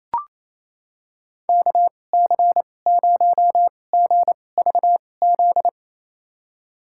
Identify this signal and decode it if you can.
Morse code, heard as a keyed tone, about 27 wpm, spaced as for 20 wpm, 700 Hz. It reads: KC0GVZ
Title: USA Call Signs - 6 Characters - 2x Character Spacing - 28wpm